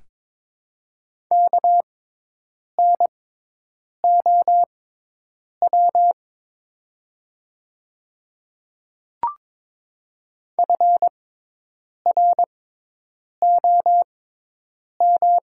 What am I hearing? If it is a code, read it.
Morse code, heard as a keyed tone, about 22 wpm, spaced as for 8 wpm, 700 Hz. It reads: KNOW FROM
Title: 6x Character Spacing - Top 100 Words - 22wpm